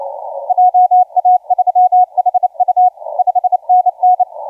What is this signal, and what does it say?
Morse code, heard as a keyed tone, about 28 wpm, 730 Hz, with radio noise. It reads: JA3HU 5NN
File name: T88TV 0748UTC 15m